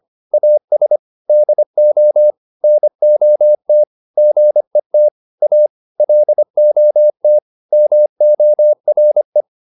Morse code, heard as a keyed tone, about 25 wpm, 600 Hz. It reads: AS DO NOT GET A LOT MORE
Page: advanced-practice-files